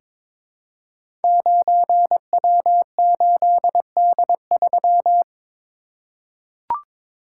Morse code, heard as a keyed tone, about 22 wpm, 700 Hz. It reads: 9W8D3
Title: Code Groups - 5 Letter:Numbers - 22wpm